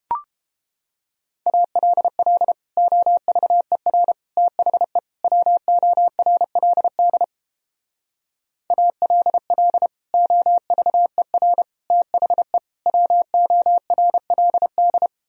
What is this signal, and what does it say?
Morse code, heard as a keyed tone, about 32 wpm, 700 Hz. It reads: ALL OVER THE WORLD ALL OVER THE WORLD